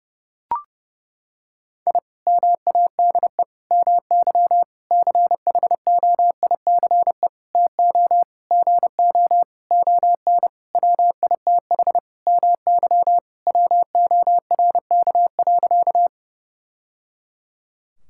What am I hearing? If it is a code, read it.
Morse code, heard as a keyed tone, about 30 wpm, 700 Hz. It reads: I MADE MY CHOICE TO GO ON WITH MY WORK.
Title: Sentences from Top 1000 Words - Review Part 6 - 30wpm